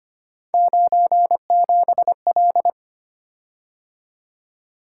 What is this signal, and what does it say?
Morse code, heard as a keyed tone, about 25 wpm, 700 Hz. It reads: 97L
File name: Three Letter-Number Combinations - 25wpm